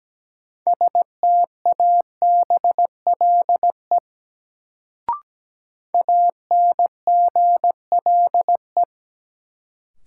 Morse code, heard as a keyed tone, about 17 wpm, 700 Hz. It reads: STABLE ANGLE
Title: Sentences from Top 1500 Words - New Word Drill Alt - 17wpm